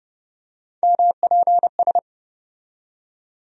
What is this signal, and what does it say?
Morse code, heard as a keyed tone, about 30 wpm, 700 Hz. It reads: MPS
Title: USA Call Sign Postfixes - 3 Letters 30wpm